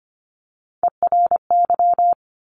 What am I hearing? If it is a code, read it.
Morse code, heard as a keyed tone, about 25 wpm, 700 Hz. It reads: ERY
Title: Top Three Letter Combinations in Words 25wpm